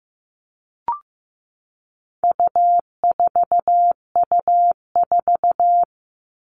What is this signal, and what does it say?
Morse code, heard as a keyed tone, about 15 wpm, 700 Hz. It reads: U4U4
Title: All Cut Numbers - 4-Digit 15wpm